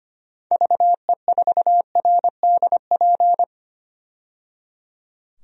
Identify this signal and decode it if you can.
Morse code, heard as a keyed tone, about 25 wpm, 700 Hz. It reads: VE4RDP
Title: DXCC-100 Call Signs - 6 Characters - 25wpm